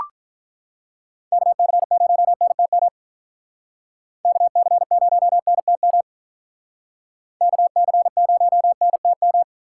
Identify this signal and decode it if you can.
Morse code, heard as a keyed tone, about 44 wpm, 700 Hz. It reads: KC0NTM KC0NTM KC0NTM